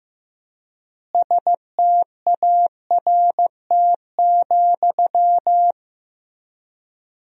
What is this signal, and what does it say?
Morse code, heard as a keyed tone, about 15 wpm, 700 Hz. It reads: START,
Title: 3x Word-Spacing - Sentences from Top 300 Words - 15wpm